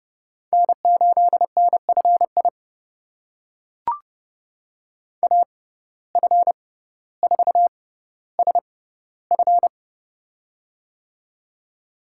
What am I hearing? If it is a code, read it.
Morse code, heard as a keyed tone, about 29 wpm, spaced as for 15 wpm, 700 Hz. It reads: N8NFI AF4SF
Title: USA Call Signs - 5 Characters - 6x Character Spacing - 30wpm